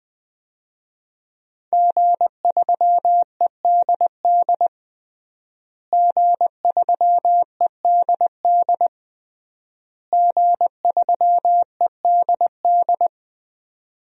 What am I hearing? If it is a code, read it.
Morse code, heard as a keyed tone, about 20 wpm, 700 Hz. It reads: G3EDD G3EDD G3EDD
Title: DXCC-100 Call Signs - 5 Characters - Repeated 3x - 20wpm